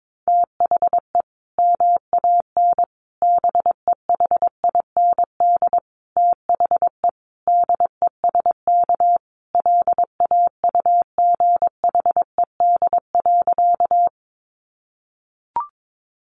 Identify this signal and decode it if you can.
Morse code, heard as a keyed tone, about 22 wpm, 700 Hz. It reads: THE MAN BEHIND THE DESK LAUGHED.
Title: A Sound Of Thunder 22wpm